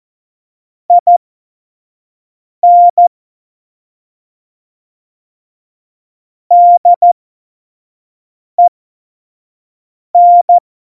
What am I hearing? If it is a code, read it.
Morse code, heard as a keyed tone, about 14 wpm, spaced as for 5 wpm, 700 Hz. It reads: IN DEN
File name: Morsekurs / Witze / Witz 22. August 17, Tempo 25 F70